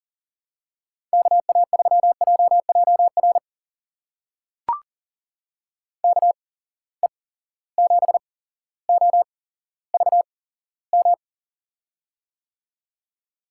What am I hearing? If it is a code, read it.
Morse code, heard as a keyed tone, about 38 wpm, spaced as for 17 wpm, 700 Hz. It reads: KA3JJR KE7OVM
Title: USA Call Signs - 6 Characters - 8x Character Spacing - 40wpm